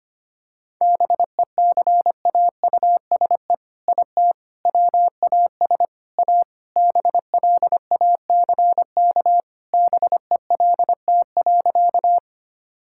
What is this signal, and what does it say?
Morse code, heard as a keyed tone, about 25 wpm, 700 Hz. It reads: BECAUSE IT WAS A BLACK BELT.
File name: practice / 25wpm / All-Age Appropriate Jokes - 25wpm